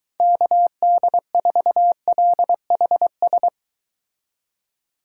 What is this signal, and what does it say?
Morse code, heard as a keyed tone, about 23 wpm, 700 Hz. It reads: KD4LHS